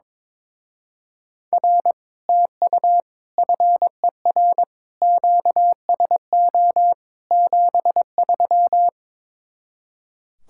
Morse code, heard as a keyed tone, about 23 wpm, 700 Hz. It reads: R TU FER QSO 73
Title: Speed-Racing Common QSO Phrases Encore - 22wpm